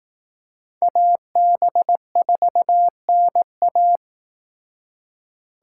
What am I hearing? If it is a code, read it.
Morse code, heard as a keyed tone, about 18 wpm, 700 Hz. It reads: AB4NA
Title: Speed Racing - USA 2x2 Call Signs 15wpm